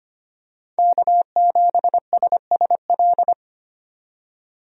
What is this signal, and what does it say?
Morse code, heard as a keyed tone, about 25 wpm, 700 Hz. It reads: K7SSL
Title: USA Call Signs - 5 Characters - 25wpm